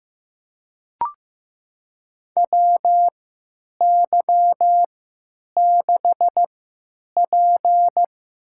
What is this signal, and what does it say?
Morse code, heard as a keyed tone, about 15 wpm, spaced as for 8 wpm, 700 Hz. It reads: WY6P
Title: USA Call Signs - 4 Characters - 3x Character Spacing - 15wpm